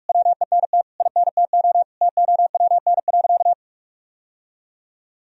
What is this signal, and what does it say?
Morse code, heard as a keyed tone, about 46 wpm, 700 Hz. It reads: WENT INTO TOWN.
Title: Sentences from Top 1000 Words - Review Part 1 - 45wpm